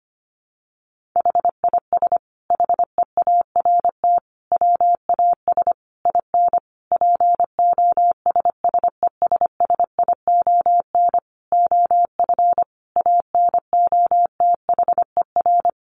Morse code, heard as a keyed tone, about 25 wpm, 700 Hz. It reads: HIS HEART WAS IN POSSESSION OF ANOTHER
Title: Sentences from Top 1400 Words 25wpm